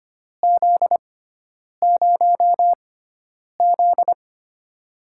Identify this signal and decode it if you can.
Morse code, heard as a keyed tone, about 25 wpm, spaced as for 9 wpm, 700 Hz. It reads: Z0Z
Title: USA Call Signs - 4 Characters - 6x Character Spacing - 25wpm